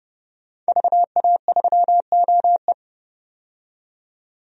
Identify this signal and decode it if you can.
Morse code, heard as a keyed tone, about 30 wpm, 690 Hz. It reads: VA3OE
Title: DXCC-100 Call Signs - 5 Characters - 30wpm